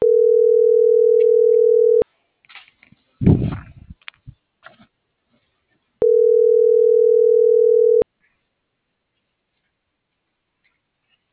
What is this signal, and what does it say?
Ringback tone (North America), 440 + 480 Hz.